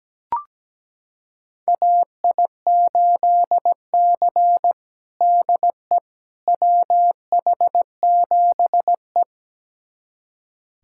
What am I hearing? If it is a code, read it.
Morse code, heard as a keyed tone, about 17 wpm, 700 Hz. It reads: AI8C DE WH7E
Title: USA Callsign DE Callsign - 4 Characters - 17wpm